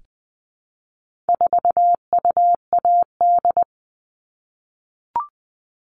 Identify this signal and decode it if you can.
Morse code, heard as a keyed tone, about 20 wpm, 700 Hz. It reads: 4UAD
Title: All Cut Numbers - 4-Digit 20wpm